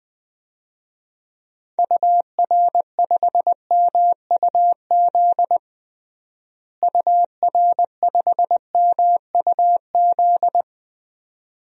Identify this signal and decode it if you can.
Morse code, heard as a keyed tone, about 20 wpm, 700 Hz. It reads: UR5MUZ UR5MUZ
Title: DXCC-100 Call Signs - 6 Characters - Repeated 2x - 20wpm